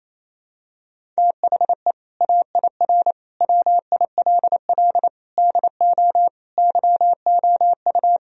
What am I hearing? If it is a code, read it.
Morse code, heard as a keyed tone, about 28 wpm, 700 Hz. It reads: THE AIR WILL DO YOU
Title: Sentences from Top 1000 Words - Review Part 1 - 28wpm